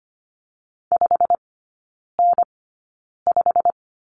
Morse code, heard as a keyed tone, about 25 wpm, spaced as for 9 wpm, 700 Hz. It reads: 5N5
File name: course / 25:10wpm / 093 Lesson 16e - Letter c - Three Characters 25wpm